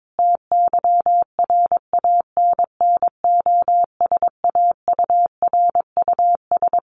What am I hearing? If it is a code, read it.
Morse code, heard as a keyed tone, about 22 wpm, 700 Hz. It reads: TYRANNOSAURUS
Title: A Sound Of Thunder 22wpm